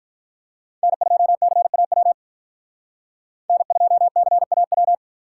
Morse code, heard as a keyed tone, about 46 wpm, 700 Hz. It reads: N2CAW N2CAW